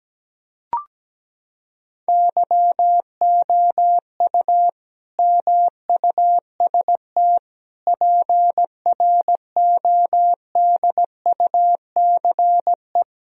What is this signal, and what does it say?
Morse code, heard as a keyed tone, about 17 wpm, 700 Hz. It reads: YOU MUST PRODUCE
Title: Sentences from Top 1000 Words - Review Part 3 - 17wpm